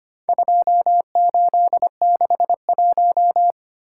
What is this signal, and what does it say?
Morse code, heard as a keyed tone, about 25 wpm, 700 Hz. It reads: 2861
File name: practice / 25wpm / Four Letter-Number Combinations - 25wpm